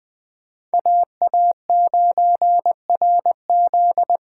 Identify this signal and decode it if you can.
Morse code, heard as a keyed tone, about 24 wpm, 700 Hz. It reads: AA9RZ
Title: Speed Racing - USA 2x2 Call Signs 20wpm